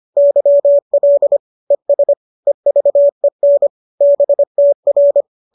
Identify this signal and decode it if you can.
Morse code, heard as a keyed tone, about 25 wpm, 570 Hz. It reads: YL ES EVEN BTR